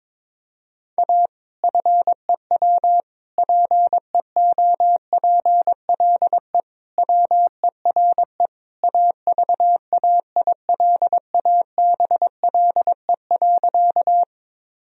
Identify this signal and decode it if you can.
Morse code, heard as a keyed tone, about 22 wpm, 700 Hz. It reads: A FEW PEOPLE WERE AVAILABLE.
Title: Sentences from Top 1000 Words - Review Part 3 - 22wpm